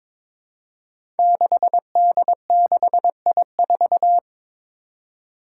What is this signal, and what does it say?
Morse code, heard as a keyed tone, about 22 wpm, 700 Hz. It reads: 6D6I4
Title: Five Challenging Letter-Number Combinations 22wpm